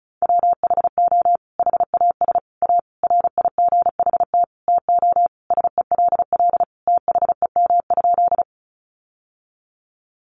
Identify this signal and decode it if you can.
Morse code, heard as a keyed tone, about 34 wpm, 700 Hz. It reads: WHO HAS A RIGHT TO SELL THEM?
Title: Sentences from Top 1100 Words 35wpm